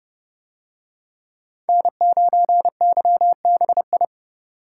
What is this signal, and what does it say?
Morse code, heard as a keyed tone, about 29 wpm, 700 Hz. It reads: N9YBI